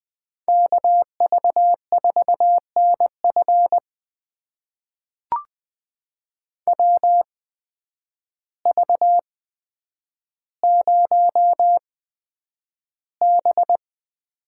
Morse code, heard as a keyed tone, about 20 wpm, spaced as for 9 wpm, 700 Hz. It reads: KV4NF WV0B
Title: USA Call Signs - 8x Character Spacing - 20wpm